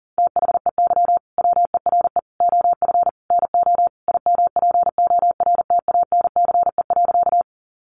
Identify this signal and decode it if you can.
Morse code, heard as a keyed tone, about 40 wpm, 700 Hz. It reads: THEY WERE OF NO IMPORTANCE.
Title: Sentences from Top 1200 Words 40wpm